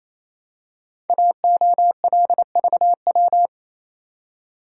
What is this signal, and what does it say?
Morse code, heard as a keyed tone, about 27 wpm, 700 Hz. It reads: AOLVW
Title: Code Groups - 5 Letters - 28wpm